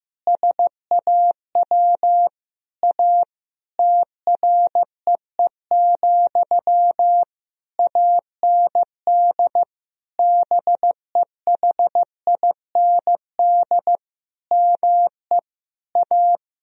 Morse code, heard as a keyed tone, about 15 wpm, 700 Hz. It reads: SAW A TREE, AND BEHIND ME A
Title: Sentences from Top 1000 Words - Review Part 2 - 15wpm